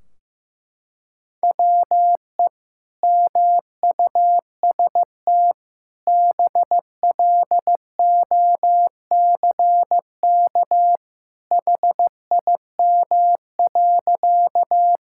Morse code, heard as a keyed tone, about 16 wpm, 700 Hz. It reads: WE MUST BLOCK HIM.
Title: Sentences from Top 1600 Words 15wpm